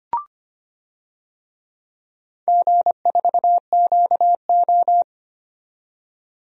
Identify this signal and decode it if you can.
Morse code, heard as a keyed tone, about 25 wpm, 700 Hz. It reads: G4QO
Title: DXCC-100 Call Signs - 4 Characters - 25wpm